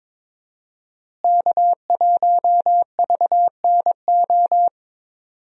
Morse code, heard as a keyed tone, about 22 wpm, 700 Hz. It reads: K1VNO
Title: USA Call Signs - 5 Characters - Repeated 2x - 22wpm